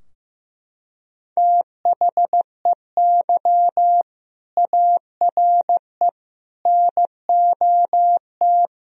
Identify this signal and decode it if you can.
Morse code, heard as a keyed tone, about 15 wpm, 700 Hz. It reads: THEY ARE NOT
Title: Sentences from Top 1000 Words - Review Part 3 - 15wpm